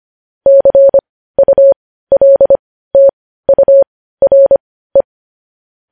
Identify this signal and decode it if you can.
Morse code, heard as a keyed tone, about 25 wpm, spaced as for 15 wpm, 550 Hz. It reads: CULTURE
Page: news-headlines-in-morse-code-at-15-wpm